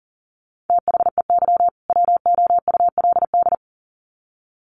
Morse code, heard as a keyed tone, about 40 wpm, 700 Hz. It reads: THEY WOULD
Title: Sets of 2 Words 40wpm